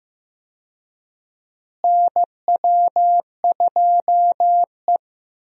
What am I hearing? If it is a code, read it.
Morse code, heard as a keyed tone, about 15 wpm, 700 Hz. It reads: NW2E
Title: USA Call Signs - 4 Characters - Repeated 3x - 15wpm